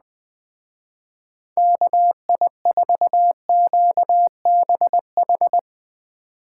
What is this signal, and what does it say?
Morse code, heard as a keyed tone, about 22 wpm, 700 Hz. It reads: KI4QBH